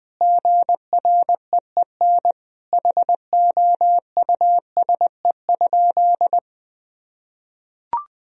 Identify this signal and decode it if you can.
Morse code, heard as a keyed tone, about 20 wpm, 700 Hz. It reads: GREEN HOUSE?
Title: Sentences from Top 1000 Words - Review Part 2 - 20wpm